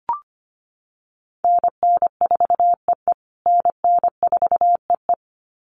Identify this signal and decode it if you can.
Morse code, heard as a keyed tone, about 25 wpm, 700 Hz. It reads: NN4EE NN4EE
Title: USA Call Signs Repeated 2x 25wpm